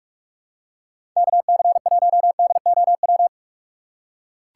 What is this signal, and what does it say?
Morse code, heard as a keyed tone, about 42 wpm, 700 Hz. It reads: KC1DOW
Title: USA Call Signs - 6 Characters - 3x Character Spacing - 45wpm